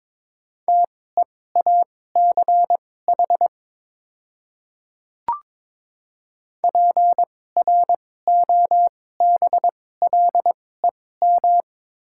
Morse code, heard as a keyed tone, about 22 wpm, spaced as for 16 wpm, 700 Hz. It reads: TEACH PROBLEM
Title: 2x Character Spacing - Top 500 Words - 22wpm